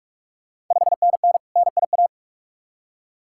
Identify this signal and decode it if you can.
Morse code, heard as a keyed tone, about 44 wpm, 690 Hz. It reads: 5NN NIA
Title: Ontario QSO Party Exchange 45wpm